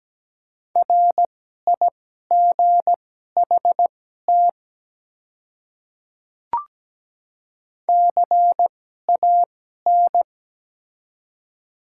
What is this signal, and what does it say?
Morse code, heard as a keyed tone, about 17 wpm, spaced as for 12 wpm, 700 Hz. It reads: RIGHT CAN
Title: 2x Character Spacing - Top 200 Words - 17wpm